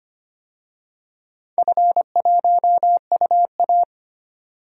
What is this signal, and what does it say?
Morse code, heard as a keyed tone, about 25 wpm, 700 Hz. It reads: F1UA
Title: DXCC-100 Call Signs - 4 Characters - 25wpm